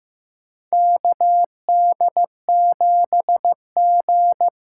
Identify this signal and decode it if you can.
Morse code, heard as a keyed tone, about 15 wpm, 700 Hz. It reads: KD7G
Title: USA Call Signs - 4 Characters - 3x Character Spacing - 15wpm